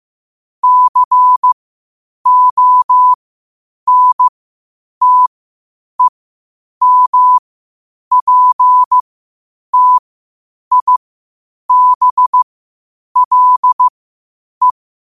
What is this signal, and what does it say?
Morse code, heard as a keyed tone, about 15 wpm, spaced as for 8 wpm, 1000 Hz. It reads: CONTEMPTIBLE
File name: qotdfiles / 8wpm